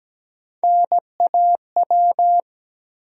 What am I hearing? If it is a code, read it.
Morse code, heard as a keyed tone, about 17 wpm, 700 Hz. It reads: NAW
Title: USA Call Sign Postfixes - 3 Letters 17wpm